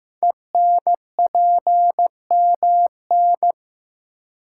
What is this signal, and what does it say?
Morse code, heard as a keyed tone, about 15 wpm, 700 Hz. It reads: ENPMN
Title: Code Groups - 5 Letter:Numbers - 15wpm